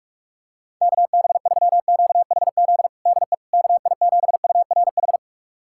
Morse code, heard as a keyed tone, about 46 wpm, 690 Hz. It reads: KB3QHZ DE KI7URH
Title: USA Callsign DE Callsign - 6 Characters - 45wpm